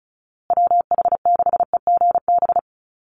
Morse code, heard as a keyed tone, about 34 wpm, 700 Hz. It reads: WH6EGB
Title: USA Call Signs Repeated 2x 35wpm